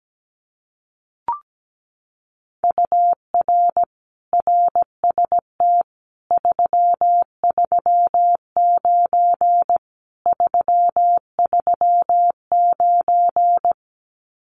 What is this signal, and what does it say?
Morse code, heard as a keyed tone, about 17 wpm, 700 Hz. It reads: UR RST 339 339
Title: Common QSO Phrases Encore 17wpm